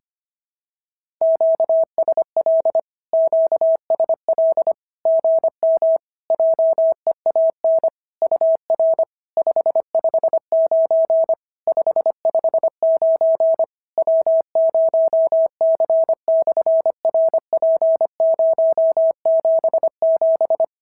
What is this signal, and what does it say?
Morse code, heard as a keyed tone, about 25 wpm, 650 Hz. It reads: QSL QSL GM JEAN UR 559 559 W0C/RP077
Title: SOTA QSO Practice 25wpm